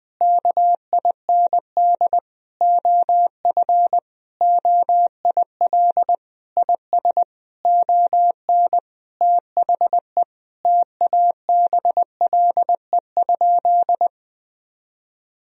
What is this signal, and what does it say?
Morse code, heard as a keyed tone, about 20 wpm, 700 Hz. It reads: KIND OF OIL IS ON THE TABLE?
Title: Sentences from Top 1000 Words - Review Part 4 - 20wpm